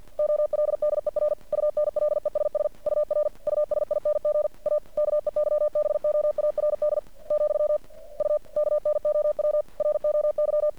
Morse code, heard as a keyed tone, about 51 wpm, 600 Hz. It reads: OLDER UNLESS UR USING A KEYBOARD , U KNOW ROY
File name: archived / CRSnet